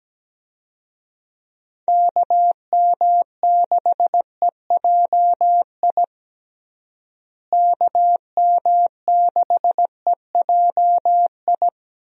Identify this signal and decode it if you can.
Morse code, heard as a keyed tone, about 17 wpm, 700 Hz. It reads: KM6EJI KM6EJI
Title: USA Call Signs - 6 Characters - Repeated 3x - 17wpm